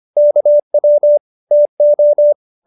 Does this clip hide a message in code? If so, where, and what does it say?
Yes: Morse code from 0.2 to 2.3 s, heard as a keyed tone, about 25 wpm, 590 Hz. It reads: KW TO